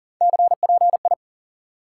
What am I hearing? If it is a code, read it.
Morse code, heard as a keyed tone, about 40 wpm, 700 Hz. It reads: CPI
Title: Top 180 QSO Elements 40wpm